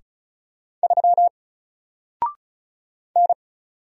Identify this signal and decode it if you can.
Morse code, heard as a keyed tone, about 36 wpm, 700 Hz. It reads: 3 N
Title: Single Letter-Number Alt 35wpm